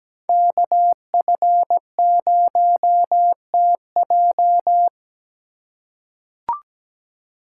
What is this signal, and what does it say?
Morse code, heard as a keyed tone, about 17 wpm, 700 Hz. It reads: KF0TJ
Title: USA Call Signs - 5 Characters - 6x Character Spacing - 17wpm